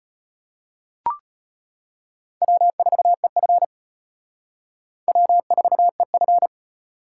Morse code, heard as a keyed tone, about 33 wpm, 700 Hz. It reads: W4EF W4EF